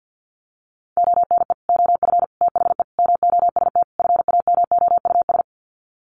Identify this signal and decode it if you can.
Morse code, heard as a keyed tone, about 48 wpm, 700 Hz. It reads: ONE OF THE MOST FAMOUS
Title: Sets of 5 Words Encore2 50wpm